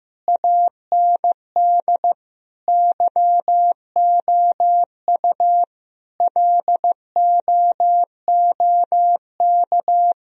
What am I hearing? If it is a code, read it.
Morse code, heard as a keyed tone, about 15 wpm, 700 Hz. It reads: AND YOU LOOK
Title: Sentences from Top 1000 Words - Review Part 4 - 15wpm